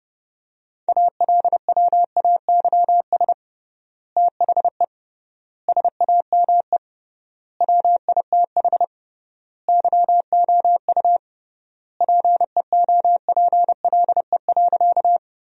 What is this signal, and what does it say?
Morse code, heard as a keyed tone, about 32 wpm, 700 Hz. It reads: ALWAYS THE SAME WITH YOU PEOPLE.
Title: 3x Word-Spacing - Sentences from Top 300 Words - 30wpm